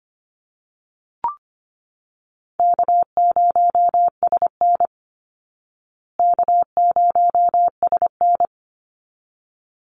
Morse code, heard as a keyed tone, about 25 wpm, 700 Hz. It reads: K0SN K0SN
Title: Easy Call Signs Repeated 2x - 25wpm